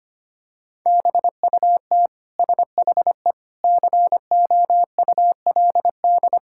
Morse code, heard as a keyed tone, about 25 wpm, 700 Hz. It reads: BUT SHE COULD
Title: Sentences from Top 1000 Words - Review Part 6 - 25wpm